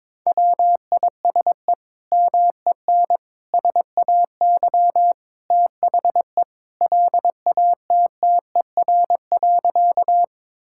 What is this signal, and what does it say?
Morse code, heard as a keyed tone, about 22 wpm, 700 Hz. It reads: WISE MEN SAY THE LATTER.
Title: Sentences from Top 1600 Words 22wpm